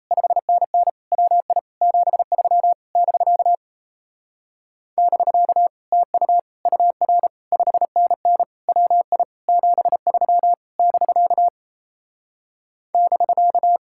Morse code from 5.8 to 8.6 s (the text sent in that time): TU UR 5NN